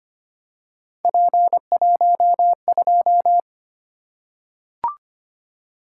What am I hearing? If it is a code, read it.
Morse code, heard as a keyed tone, about 25 wpm, 700 Hz. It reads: P12
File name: Three Letter-Number Combinations - 25wpm